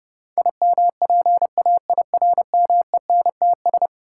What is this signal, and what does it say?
Morse code, heard as a keyed tone, about 32 wpm, 700 Hz. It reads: IMPAIRMENTS
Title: Sets of 5 Words Encore - New Word Drill - 30wpm